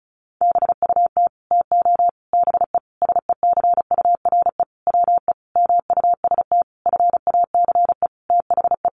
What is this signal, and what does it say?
Morse code, heard as a keyed tone, about 34 wpm, 700 Hz. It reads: BUT TO BE SECURE WE MUST FACE THE
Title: Sentences from Top 1200 Words 35wpm